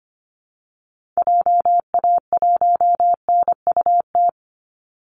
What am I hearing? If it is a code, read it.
Morse code, heard as a keyed tone, about 25 wpm, 700 Hz. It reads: JA1NUT
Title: Call Signs Repeated 2x - 25wpm